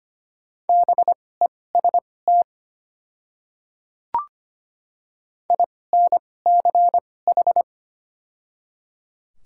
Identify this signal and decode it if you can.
Morse code, heard as a keyed tone, about 25 wpm, spaced as for 18 wpm, 690 Hz. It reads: BEST INCH
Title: 2x Character Spacing - Top 500 Words - 25wpm